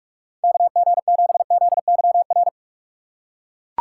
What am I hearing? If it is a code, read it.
Morse code, heard as a keyed tone, about 44 wpm, 700 Hz. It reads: KG7ZYR